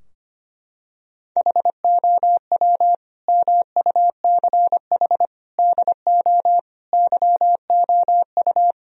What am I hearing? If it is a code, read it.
Morse code, heard as a keyed tone, about 25 wpm, 700 Hz. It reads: HOW MUCH DO YOU
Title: Sentences from Top 1000 Words - Review Part 3 - 25wpm